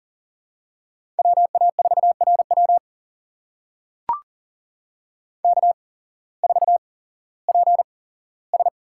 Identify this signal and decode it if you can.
Morse code, heard as a keyed tone, about 40 wpm, spaced as for 19 wpm, 700 Hz. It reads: WA4RW K4PS
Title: USA Call Signs - 5 Characters - 8x Character Spacing - 40wpm